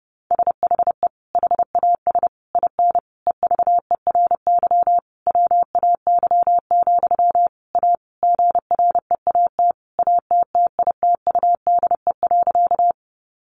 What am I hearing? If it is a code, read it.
Morse code, heard as a keyed tone, about 30 wpm, 700 Hz. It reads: SHE HAS IN EVERY WAY, A GREAT ATTITUDE.
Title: Sentences from Top 1300 Words 30wpm